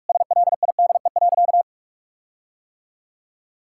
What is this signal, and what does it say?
Morse code, heard as a keyed tone, about 46 wpm, 700 Hz. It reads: SLIDE.
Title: Sentences from Top 1600 Words 45wpm